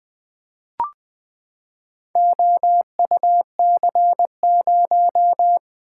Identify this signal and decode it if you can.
Morse code, heard as a keyed tone, about 20 wpm, 700 Hz. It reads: OUC0
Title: Four Letter-Number Combinations - 20wpm